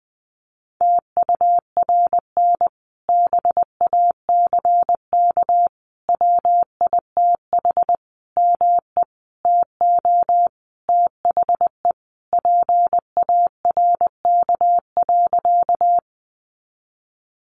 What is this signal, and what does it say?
Morse code, heard as a keyed tone, about 20 wpm, 700 Hz. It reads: TURN BACK WITH ME TO THE PARK.
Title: Sentences from Top 1300 Words 20wpm